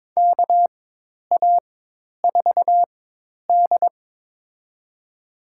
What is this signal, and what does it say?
Morse code, heard as a keyed tone, about 22 wpm, spaced as for 10 wpm, 700 Hz. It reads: KA4D
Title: USA Call Signs - 4 Characters - 4x Character Spacing - 22wpm